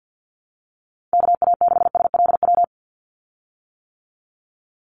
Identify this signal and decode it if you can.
Morse code, heard as a keyed tone, about 50 wpm, 700 Hz. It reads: KA6SLW
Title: USA Call Signs 50wpm